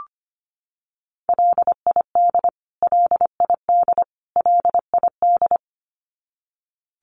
Morse code, heard as a keyed tone, about 25 wpm, 700 Hz. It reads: LID LID LID
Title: Abbreviations Repeated 3x - 25wpm